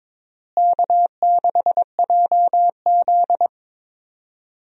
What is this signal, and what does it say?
Morse code, heard as a keyed tone, about 22 wpm, 700 Hz. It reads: K6JZ
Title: USA Call Signs - 4 Characters - Repeated 2x - 22wpm